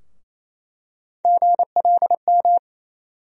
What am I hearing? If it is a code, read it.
Morse code, heard as a keyed tone, about 27 wpm, 700 Hz. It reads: GLM
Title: USA Call Sign Postfixes - 3 Letters 28wpm